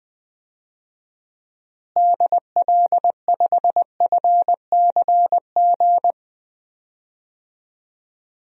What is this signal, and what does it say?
Morse code, heard as a keyed tone, about 20 wpm, 700 Hz. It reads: DL5FCG
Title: DXCC-100 Call Signs - 6 Characters - Repeated 2x - 20wpm